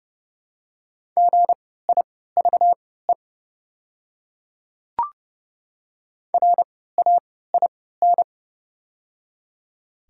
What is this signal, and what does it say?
Morse code, heard as a keyed tone, about 30 wpm, spaced as for 17 wpm, 700 Hz. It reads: GIVE RAIN
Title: 3x Character Spacing - Top 300 Words - 30wpm